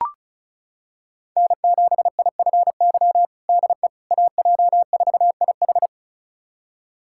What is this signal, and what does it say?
Morse code, heard as a keyed tone, about 36 wpm, 700 Hz. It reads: N7IFY DE AJ4IH
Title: USA Callsign DE Callsign - 5 Characters - 35wpm